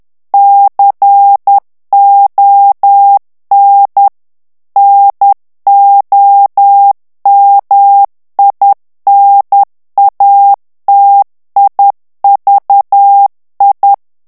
Morse code, heard as a keyed tone, about 10 wpm, 800 Hz. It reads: CON NOMINATIVI
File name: ITALIEN texte clair 2 vitesse 10 mots minute